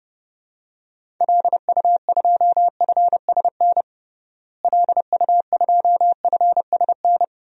Morse code, heard as a keyed tone, about 30 wpm, 700 Hz. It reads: LU2FSN LU2FSN
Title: DXCC-100 Call Signs - 6 Characters - Repeated 3x - 30wpm